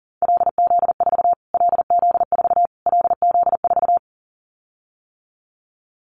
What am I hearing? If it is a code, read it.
Morse code, heard as a keyed tone, about 40 wpm, 700 Hz. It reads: LZ4 LZ4 LZ4
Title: Callsign Prefixes for Top 25 DXCC - Repeated 3x - 40wpm